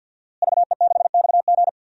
Morse code, heard as a keyed tone, about 47 wpm, 700 Hz. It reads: VE6XG
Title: DXCC-100 Call Signs - 5 Characters - Repeated 2x - 50wpm